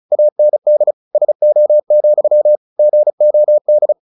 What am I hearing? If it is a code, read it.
Morse code, heard as a keyed tone, about 34 wpm, 600 Hz. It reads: AND SO, GOD
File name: the old man and the sea / the old man and the sea 1 35wpm